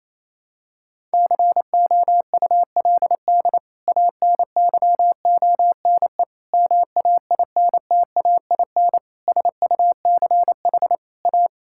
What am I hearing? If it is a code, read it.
Morse code, heard as a keyed tone, about 27 wpm, 700 Hz. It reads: COULD ANYONE MAINTAIN SUCH A
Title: Sentences from Top 1000 Words - Review Part 5 - 28wpm